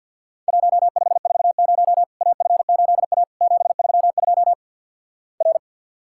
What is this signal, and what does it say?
Morse code, heard as a keyed tone, about 50 wpm, 700 Hz. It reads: 1540 AF8A 732 R
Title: Worked All Europe DX Contest - QTC Exchange with Repeats - 50wpm